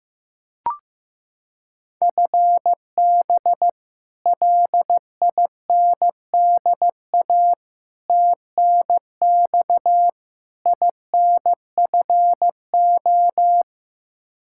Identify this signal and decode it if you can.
Morse code, heard as a keyed tone, about 15 wpm, 700 Hz. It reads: FB LINDA TNX INFO
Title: Practice QSO - Exchange 3 - More Info - 15wpm